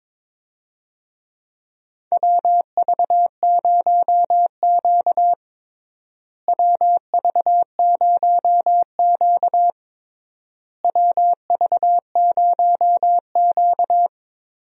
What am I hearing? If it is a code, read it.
Morse code, heard as a keyed tone, about 21 wpm, 700 Hz. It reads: WV0Q WV0Q WV0Q
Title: USA Call Signs - 4 Characters - Repeated 3x - 22wpm